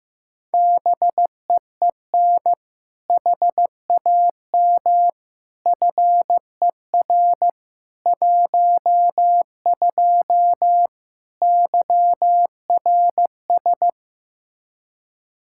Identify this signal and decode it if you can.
Morse code, heard as a keyed tone, about 15 wpm, 700 Hz. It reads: BEEN HAM FER 12 YRS
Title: Practice QSO - Exchange 3 - More Info - 15wpm